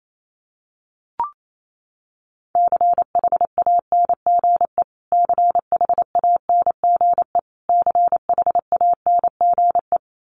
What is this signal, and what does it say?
Morse code, heard as a keyed tone, about 28 wpm, 700 Hz. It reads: CHANGE CHANGE CHANGE
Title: Top 200 Words Repeated 28wpm